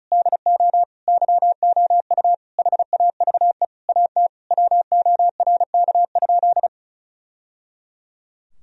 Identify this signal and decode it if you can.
Morse code, heard as a keyed tone, about 36 wpm, 700 Hz. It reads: DO YOU HAVE AT WORK?
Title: Sentences from Top 1000 Words - Review Part 4 - 35wpm